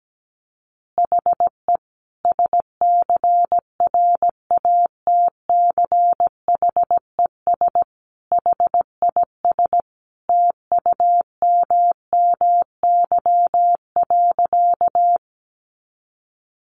Morse code, heard as a keyed tone, about 17 wpm, 700 Hz. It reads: HE SCRATCHES HIS TUMMY.
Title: Easy Phrases 17wpm